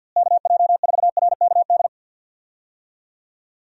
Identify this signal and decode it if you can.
Morse code, heard as a keyed tone, about 47 wpm, 700 Hz. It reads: KJ4RKD
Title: USA Call Signs - 6 Characters - Repeated 3x - 50wpm